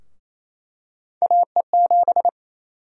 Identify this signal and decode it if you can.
Morse code, heard as a keyed tone, about 28 wpm, 700 Hz. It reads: AE7
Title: USA Call Sign Prefixes - 28wpm